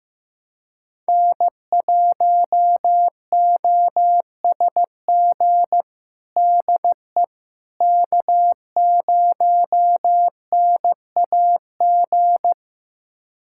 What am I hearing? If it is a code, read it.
Morse code, heard as a keyed tone, about 15 wpm, 700 Hz. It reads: N1OSG DE K0NAG
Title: USA Callsign DE Callsign - 5 Characters - 15wpm